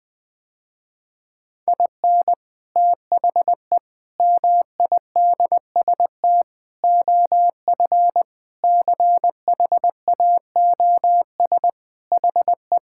Morse code, heard as a keyed tone, about 20 wpm, 700 Hz. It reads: IN THE MIDST OF CHAOS HE
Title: Sentences with 4-Word Phrases - Spoken Beginning - 20wpm